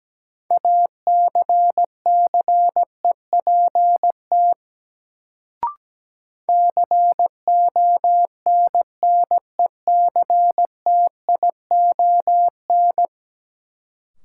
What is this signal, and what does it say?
Morse code, heard as a keyed tone, about 17 wpm, 700 Hz. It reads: ACCEPT CONNECTION
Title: Sentences from Top 1300 Words - New Word Drill - 17wpm